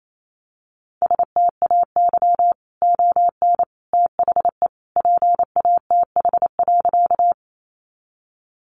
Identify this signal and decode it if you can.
Morse code, heard as a keyed tone, about 28 wpm, 700 Hz. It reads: STAY ON THE PATH.
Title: A Sound Of Thunder 28wpm